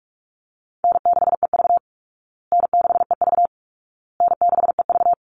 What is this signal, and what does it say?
Morse code, heard as a keyed tone, about 44 wpm, 700 Hz. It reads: N6EV N6EV N6EV
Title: Easy Call Signs Repeated 3x - 45wpm